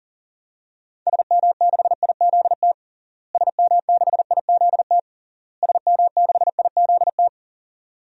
Morse code, heard as a keyed tone, about 40 wpm, 700 Hz. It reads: SM6IZT SM6IZT SM6IZT
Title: DXCC-100 Call Signs - 6 Characters - Repeated 3x - 40wpm